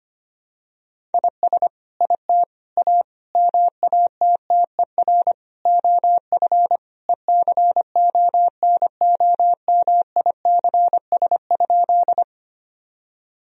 Morse code, heard as a keyed tone, about 25 wpm, 700 Hz. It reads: IS IT A MATTER OF ECONOMICS?
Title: Sentences from Top 1000 Words - Review Part 3 - 25wpm